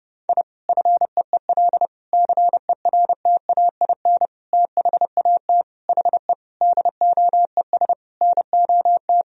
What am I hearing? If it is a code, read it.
Morse code, heard as a keyed tone, about 31 wpm, 700 Hz. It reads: I FEEL CERTAIN THAT HE DOES NOT
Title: Sentences from Top 1000 Words - Review Part 2 - 30wpm